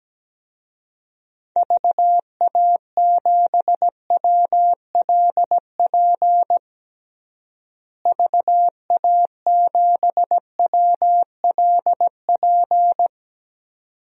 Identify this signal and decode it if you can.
Morse code, heard as a keyed tone, about 17 wpm, 700 Hz. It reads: VA7WLP VA7WLP
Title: DXCC-100 Call Signs - 6 Characters - Repeated 2x - 17wpm